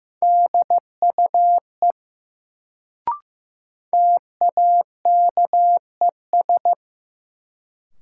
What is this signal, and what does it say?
Morse code, heard as a keyed tone, about 15 wpm, 700 Hz. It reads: DUE TAKES
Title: Sets of 5 Words Encore - New Word Drill - 15wpm